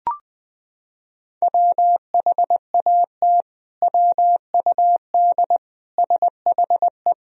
Morse code, heard as a keyed tone, about 21 wpm, 700 Hz. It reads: WHAT WUD SHE
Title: Sentences from Top 100 Words and Abbreviations 20wpm